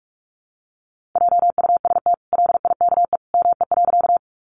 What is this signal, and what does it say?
Morse code, heard as a keyed tone, about 44 wpm, 700 Hz. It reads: JUST LIKE ME.
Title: Sentences from Top 200 Words 45wpm